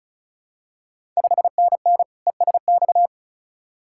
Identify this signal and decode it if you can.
Morse code, heard as a keyed tone, about 34 wpm, 700 Hz. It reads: 5NN ESX
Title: Ontario QSO Party Exchange 35wpm